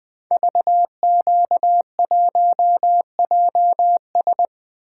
Morse code, heard as a keyed tone, about 20 wpm, 700 Hz. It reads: VQ1JS
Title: Code Groups - 5 Letter:Numbers - 20wpm